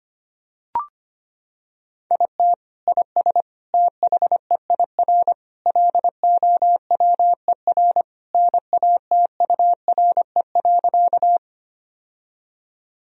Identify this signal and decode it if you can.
Morse code, heard as a keyed tone, about 25 wpm, 700 Hz. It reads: IT IS THEIR LOWER NATURE.
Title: Sentences from Top 1000 Words - Review Part 4 - 25wpm